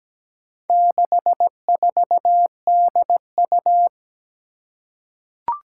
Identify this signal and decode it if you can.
Morse code, heard as a keyed tone, about 17 wpm, 700 Hz. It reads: 64DU
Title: Four Challenging Letter-Number Combinations 17wpm